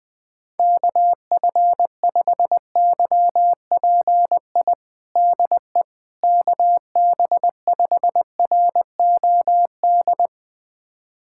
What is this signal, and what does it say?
Morse code, heard as a keyed tone, about 20 wpm, 690 Hz. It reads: KF5YPI DE KB5ROD
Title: USA Callsign DE Callsign 20wpm